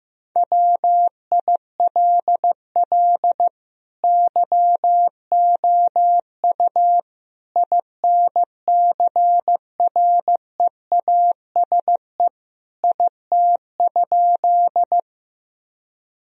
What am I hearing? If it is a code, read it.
Morse code, heard as a keyed tone, about 15 wpm, 700 Hz. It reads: WILL YOU INCREASE IT?
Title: Sentences from Top 1000 Words - Review Part 4 - 15wpm